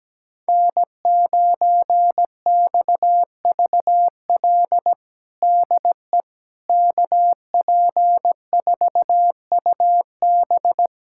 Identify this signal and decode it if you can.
Morse code, heard as a keyed tone, about 17 wpm, 700 Hz. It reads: N9XVL DE KP4UB
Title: USA Callsign DE Callsign 17wpm